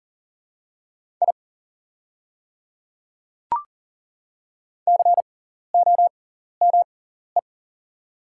Morse code, heard as a keyed tone, about 40 wpm, spaced as for 14 wpm, 700 Hz. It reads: I COME
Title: 6x Character Spacing - Top 200 Words - 40wpm